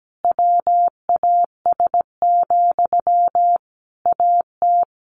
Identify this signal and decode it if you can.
Morse code, heard as a keyed tone, about 17 wpm, 700 Hz. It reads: WAS, AT
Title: Sentences from Top 1400 Words 17wpm